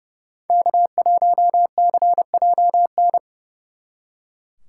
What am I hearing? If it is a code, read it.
Morse code, heard as a keyed tone, about 39 wpm, 700 Hz. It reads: K1CJN